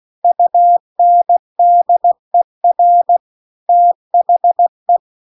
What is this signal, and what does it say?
Morse code, heard as a keyed tone, about 16 wpm, 700 Hz. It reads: UNDER THE